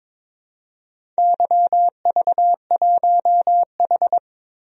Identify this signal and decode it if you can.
Morse code, heard as a keyed tone, about 22 wpm, 700 Hz. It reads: YV1H
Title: DXCC-100 Call Signs - 4 Characters - 22wpm